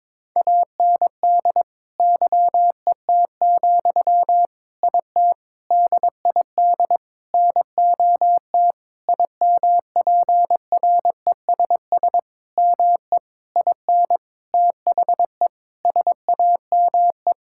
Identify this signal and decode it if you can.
Morse code, heard as a keyed tone, about 21 wpm, 700 Hz. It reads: AND YET, IT DID NOT IMPRESS ME IN THE SAME
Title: Sentences from Top 1600 Words 22wpm